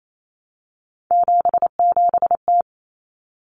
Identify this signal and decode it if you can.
Morse code, heard as a keyed tone, about 28 wpm, 700 Hz. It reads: 77T
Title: Common Cut Numbers - 3-Digit 28wpm